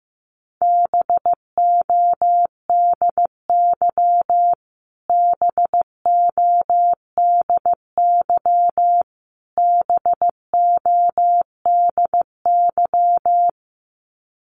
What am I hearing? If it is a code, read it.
Morse code, heard as a keyed tone, about 15 wpm, 700 Hz. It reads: BODY BODY BODY
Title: Top 500 Words Repeated 15wpm